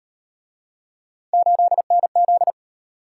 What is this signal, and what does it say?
Morse code, heard as a keyed tone, about 38 wpm, 700 Hz. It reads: 8NZ